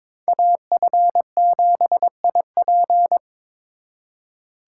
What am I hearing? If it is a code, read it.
Morse code, heard as a keyed tone, about 26 wpm, 700 Hz. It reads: AF7IP
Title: Speed Racing - USA 2x2 Call Signs 22wpm